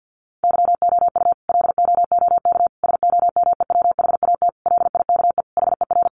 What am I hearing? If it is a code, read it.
Morse code, heard as a keyed tone, about 50 wpm, 700 Hz. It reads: YOU LOOK SOMEWHAT LIKE HER
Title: Sentences from Top 1300 Words 50wpm